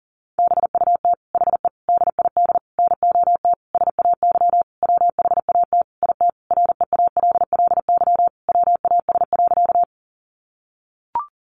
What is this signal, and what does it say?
Morse code, heard as a keyed tone, about 40 wpm, 700 Hz. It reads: BUT HE DID NOT SAY WHAT IT REALLY WAS.
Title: Sentences from Top 600 Words 40wpm